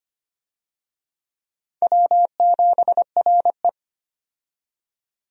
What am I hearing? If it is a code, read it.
Morse code, heard as a keyed tone, about 25 wpm, 700 Hz. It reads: W7RE
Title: USA Call Signs - 4 Characters - 25wpm